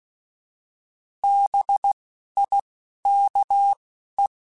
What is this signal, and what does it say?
Morse code, heard as a keyed tone, about 16 wpm, 800 Hz. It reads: B I K E